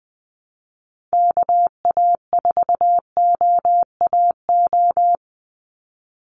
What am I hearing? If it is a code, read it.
Morse code, heard as a keyed tone, about 20 wpm, 700 Hz. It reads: KA4OAO
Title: USA Call Signs 20wpm